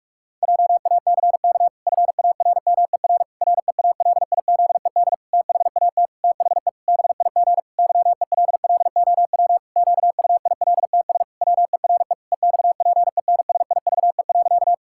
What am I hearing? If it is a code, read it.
Morse code, heard as a keyed tone, about 46 wpm, 700 Hz. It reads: JACK FARMER REALIZED THAT THE BIG YELLOW QUILTS WERE EXPENSIVE.
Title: Pangram Challenge v2 - 45wpm